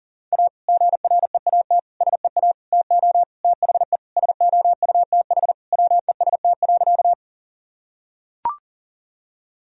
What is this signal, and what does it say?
Morse code, heard as a keyed tone, about 40 wpm, 700 Hz. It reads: A GREAT SEA TO THE SOUTH WEST.
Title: Sentences from Top 1000 Words - Review Part 2 - 40wpm